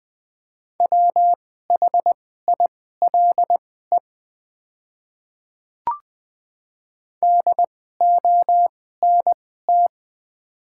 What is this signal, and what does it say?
Morse code, heard as a keyed tone, about 20 wpm, spaced as for 14 wpm, 700 Hz. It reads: WHILE DONT
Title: 2x Character Spacing - Top 300 Words - 20wpm